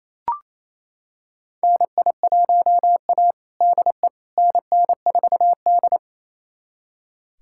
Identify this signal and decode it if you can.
Morse code, heard as a keyed tone, about 28 wpm, 700 Hz. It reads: NI1A DE NN4D
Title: USA Callsign DE Callsign - 4 Characters - 28wpm